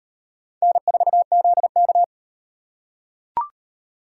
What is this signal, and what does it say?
Morse code, heard as a keyed tone, about 38 wpm, 700 Hz. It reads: N4ZK